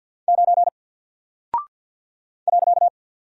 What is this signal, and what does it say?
Morse code, heard as a keyed tone, about 48 wpm, 700 Hz. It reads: 9 .
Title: Single Character (Letter:Number:Punctuation:Prosign) Alt 50wpm